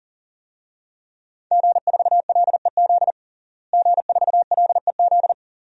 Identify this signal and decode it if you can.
Morse code, heard as a keyed tone, about 40 wpm, 700 Hz. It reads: G4LEZ G4LEZ
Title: DXCC-100 Call Signs - 5 Characters - Repeated 2x - 40wpm